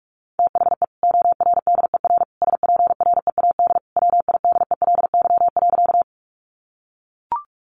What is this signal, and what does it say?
Morse code, heard as a keyed tone, about 44 wpm, 700 Hz. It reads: THE ORDER SPREAD WIDELY.
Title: Sentences from Top 1200 Words 45wpm